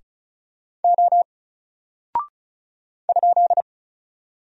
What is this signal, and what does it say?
Morse code, heard as a keyed tone, about 34 wpm, 700 Hz. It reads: O ?
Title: Single Character (Letter:Number:Punctuation:Prosign) Alt 35wpm